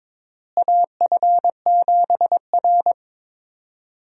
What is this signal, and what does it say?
Morse code, heard as a keyed tone, about 21 wpm, 700 Hz. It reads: AF7R
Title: USA Call Signs - 4 Characters - Repeated 2x - 22wpm